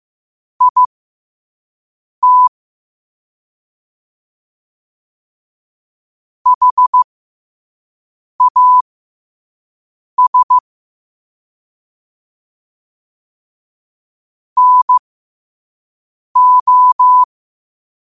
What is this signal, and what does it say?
Morse code, heard as a keyed tone, about 15 wpm, spaced as for 5 wpm, 1000 Hz. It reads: IT HAS NO
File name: qotdfiles / 5wpm